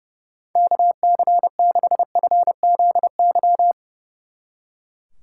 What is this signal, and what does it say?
Morse code, heard as a keyed tone, about 36 wpm, 700 Hz. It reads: KC6FZY
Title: Speed Racing - USA Call Signs 30wpm